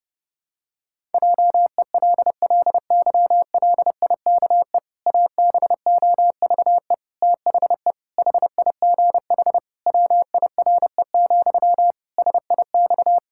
Morse code, heard as a keyed tone, about 30 wpm, 700 Hz. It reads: JELLYLIKE ABOVE THE HIGH WIRE, SIX
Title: Pangram Challenge v2 - 30wpm